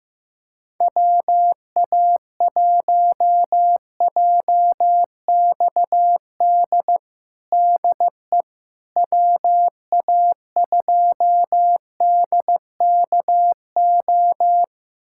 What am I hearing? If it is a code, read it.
Morse code, heard as a keyed tone, about 15 wpm, 700 Hz. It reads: WA1JXD DE WA2DKO
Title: USA Callsign DE Callsign - 6 Characters - 15wpm